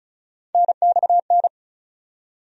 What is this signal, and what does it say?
Morse code, heard as a keyed tone, about 36 wpm, 700 Hz. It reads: NXN
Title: USA Call Sign Postfixes - 3 Letters 35wpm